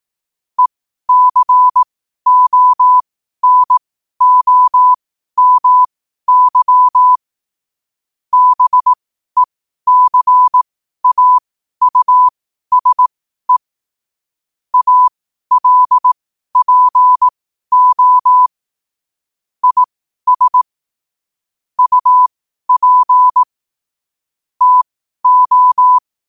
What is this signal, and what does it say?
Morse code, heard as a keyed tone, about 18 wpm, spaced as for 12 wpm, 1000 Hz. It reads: ECONOMY BECAUSE ALPO IS UP TO
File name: qotdfiles / 12wpm